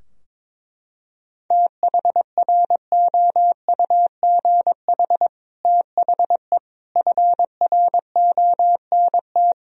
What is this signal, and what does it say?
Morse code, heard as a keyed tone, about 22 wpm, 700 Hz. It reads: THROUGH THE FRONT
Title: Sentences from Top 1000 Words - Review Part 2 - 22wpm